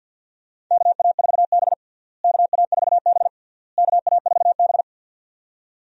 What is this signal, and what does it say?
Morse code, heard as a keyed tone, about 52 wpm, 700 Hz. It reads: KA4B KA4B KA4B
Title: USA Call Signs - 4 Characters - Repeated 3x - 50wpm